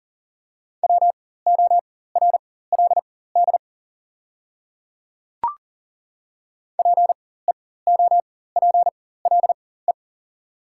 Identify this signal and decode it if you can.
Morse code, heard as a keyed tone, about 40 wpm, spaced as for 19 wpm, 700 Hz. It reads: WORLD PEOPLE
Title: 4x Character Spacing - Top 200 Words - 40wpm